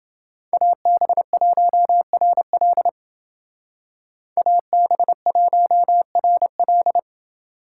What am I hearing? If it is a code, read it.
Morse code, heard as a keyed tone, about 29 wpm, 700 Hz. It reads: AB1RL AB1RL